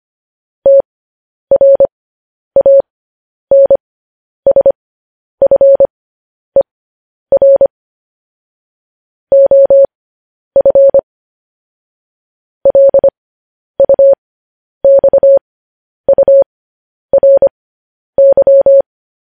Morse code, heard as a keyed tone, about 25 wpm, spaced as for 10 wpm, 550 Hz. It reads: TRANSFER OF LUXURY
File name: podcast / News headlines-1x-25x10